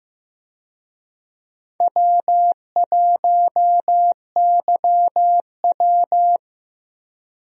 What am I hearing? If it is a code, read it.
Morse code, heard as a keyed tone, about 15 wpm, 700 Hz. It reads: W1YW
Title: USA Call Signs - 4 Characters - Repeated 2x - 15wpm